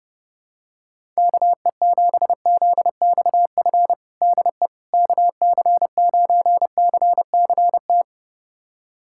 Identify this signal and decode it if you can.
Morse code, heard as a keyed tone, about 30 wpm, 700 Hz. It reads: KE7ZXF DE KC9CCT
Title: USA Callsign DE Callsign - 6 Characters - 30wpm